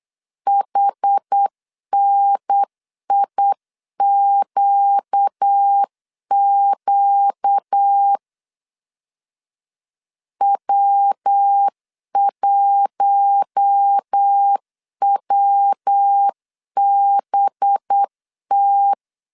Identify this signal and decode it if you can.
Morse code, heard as a keyed tone, about 8 wpm, 800 Hz. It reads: HNIQQ W1WBT
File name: CW / DET / 1 6 1500C S C 8wpm 5219